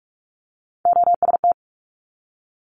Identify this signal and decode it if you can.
Morse code, heard as a keyed tone, about 46 wpm, 700 Hz. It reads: OST
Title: Multi-letter Phonemes 45wpm